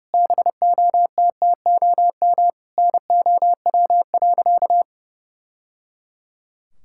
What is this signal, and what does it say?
Morse code, heard as a keyed tone, about 30 wpm, 700 Hz. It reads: BOTTOM NOW.
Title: Sentences from Top 1000 Words - Review Part 6 - 30wpm